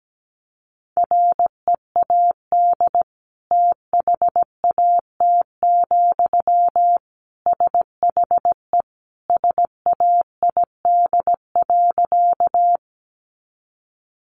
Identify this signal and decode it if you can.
Morse code, heard as a keyed tone, about 17 wpm, 700 Hz. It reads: READ THAT, SHE SAID.
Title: Sentences from Top 200 Words 17wpm